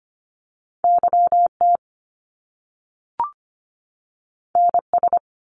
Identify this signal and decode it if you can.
Morse code, heard as a keyed tone, about 25 wpm, 700 Hz. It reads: YT NS
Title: Canadian Provinces and Territories 25wpm